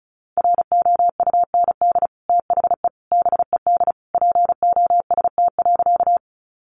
Morse code, heard as a keyed tone, about 34 wpm, 700 Hz. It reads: ROUND THE BED POST.
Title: Sentences from Top 1000 Words 35wpm